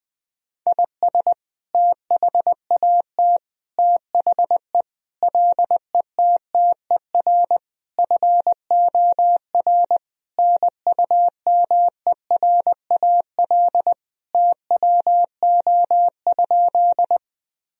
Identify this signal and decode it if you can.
Morse code, heard as a keyed tone, about 20 wpm, 700 Hz. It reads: IS THAT THE LETTER FOR NUMERAL TWO?
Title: Sentences from Top 1000 Words - Review Part 2 - 20wpm